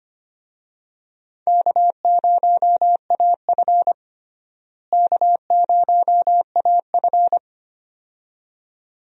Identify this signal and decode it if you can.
Morse code, heard as a keyed tone, about 25 wpm, 700 Hz. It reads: K0AF K0AF
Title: USA Call Signs - 4 Characters - Repeated 2x - 25wpm